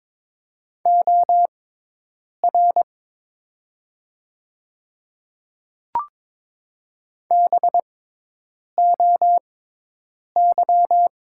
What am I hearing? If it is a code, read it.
Morse code, heard as a keyed tone, about 23 wpm, spaced as for 8 wpm, 700 Hz. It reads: OR BOY
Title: 6x Character Spacing - Top 200 Words - 22wpm